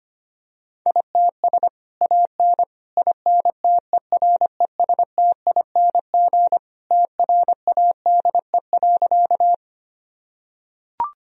Morse code, heard as a keyed tone, about 25 wpm, 700 Hz. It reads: ITS AN INTERESTING TRADE.
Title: Sentences from Top 1000 Words - Review Part 4 - 25wpm